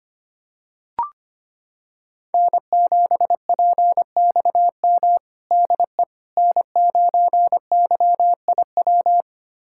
Morse code, heard as a keyed tone, about 25 wpm, 700 Hz. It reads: N7PXM DE N9YIW
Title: USA Callsign DE Callsign - 5 Characters - 25wpm